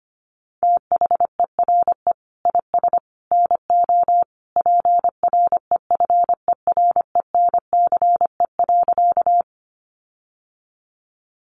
Morse code, heard as a keyed tone, about 25 wpm, 700 Hz. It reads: THERE IS NO PREFERENCE.
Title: Sentences from Top 1500 Words 25wpm